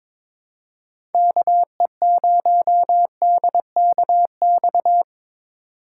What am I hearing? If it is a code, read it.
Morse code, heard as a keyed tone, about 22 wpm, 700 Hz. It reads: KE0DKX
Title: USA Call Signs - 6 Characters - 4x Character Spacing - 22wpm